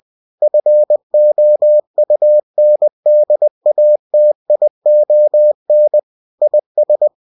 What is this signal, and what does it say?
Morse code, heard as a keyed tone, about 20 wpm, 600 Hz. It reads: FOUNDATION IS